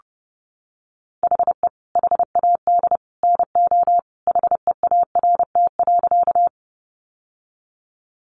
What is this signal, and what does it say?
Morse code, heard as a keyed tone, about 30 wpm, 700 Hz. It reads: HE HAD NO HEART.
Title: Sentences from Top 1200 Words 30wpm